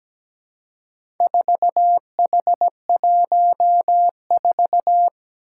Morse code, heard as a keyed tone, about 17 wpm, 700 Hz. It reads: 4H14
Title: Four Challenging Letter-Number Combinations 17wpm